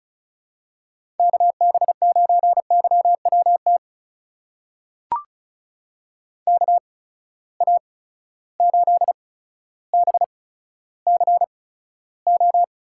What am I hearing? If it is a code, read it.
Morse code, heard as a keyed tone, about 34 wpm, spaced as for 15 wpm, 700 Hz. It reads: KB9YWT KA8BCO
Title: USA Call Signs - 6 Characters - 8x Character Spacing - 35wpm